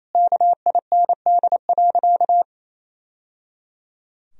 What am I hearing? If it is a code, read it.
Morse code, heard as a keyed tone, about 28 wpm, 700 Hz. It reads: KIND.
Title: Sentences from Top 1000 Words - Review Part 4 - 28wpm